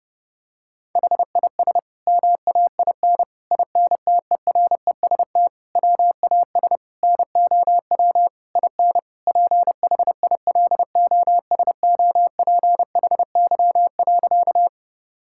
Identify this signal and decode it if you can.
Morse code, heard as a keyed tone, about 30 wpm, 700 Hz. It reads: HIS MAIN INTEREST WAS NOW IN PHILOSOPHY.
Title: Sentences from Top 1000 Words - Review Part 6 - 30wpm